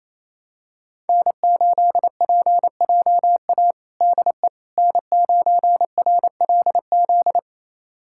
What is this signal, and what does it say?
Morse code, heard as a keyed tone, about 28 wpm, 700 Hz. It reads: N8PJA DE N9RLZ
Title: USA Callsign DE Callsign - 5 Characters - 28wpm